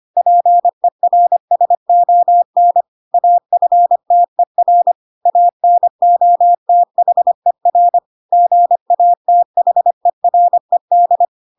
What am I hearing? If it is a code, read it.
Morse code, heard as a keyed tone, about 25 wpm, 700 Hz. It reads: PERSON AFTER ANOTHER GATHERED
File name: Pinocchio / Pinocchio ch3 25 25 ik1hge